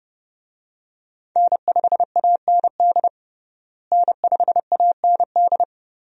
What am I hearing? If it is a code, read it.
Morse code, heard as a keyed tone, about 32 wpm, 700 Hz. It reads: N5AND N5AND
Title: USA Call Signs - 5 Characters - Repeated 2x - 30wpm